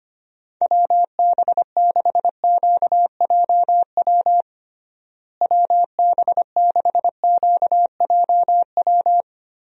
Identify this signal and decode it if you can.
Morse code, heard as a keyed tone, about 25 wpm, 700 Hz. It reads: WB6QJW WB6QJW
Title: USA Call Signs - 6 Characters - Repeated 3x - 25wpm